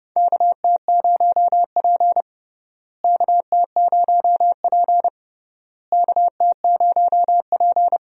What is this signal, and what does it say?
Morse code, heard as a keyed tone, about 30 wpm, 700 Hz. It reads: KT0P KT0P KT0P
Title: USA Call Signs - 4 Characters - Repeated 3x - 30wpm